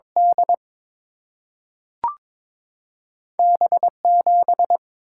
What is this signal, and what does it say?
Morse code, heard as a keyed tone, about 22 wpm, 700 Hz. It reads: D B7
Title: Single-Letter Single-Number 22wpm